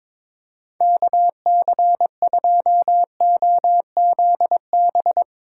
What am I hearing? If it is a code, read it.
Morse code, heard as a keyed tone, about 22 wpm, 700 Hz. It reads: KC2OZB
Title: Speed Racing - USA 2x3 Call Signs 22wpm